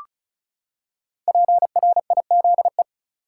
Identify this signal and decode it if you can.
Morse code, heard as a keyed tone, about 36 wpm, 700 Hz. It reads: PRIZE
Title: Five Letter Words - 35wpm